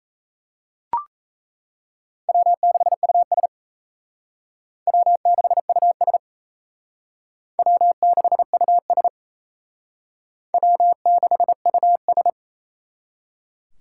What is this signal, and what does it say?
Morse code, heard as a keyed tone, about 35 wpm, 700 Hz. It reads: W6US W6US W6US W6US